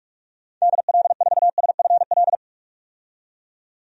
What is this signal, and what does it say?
Morse code, heard as a keyed tone, about 46 wpm, 700 Hz. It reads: DL4SFL
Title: DXCC-100 Call Signs - 6 Characters - 45wpm